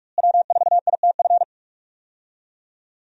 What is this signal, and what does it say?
Morse code, heard as a keyed tone, about 43 wpm, 700 Hz. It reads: W4ITF
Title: USA Call Signs - 5 Characters - Repeated 3x - 45wpm